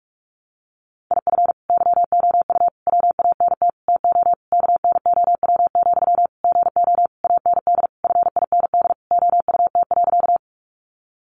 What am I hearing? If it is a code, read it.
Morse code, heard as a keyed tone, about 44 wpm, 700 Hz. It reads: IF YOU WANT TO KNOW, GO AND FIND OUT.
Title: Sentences from Top 200 Words 45wpm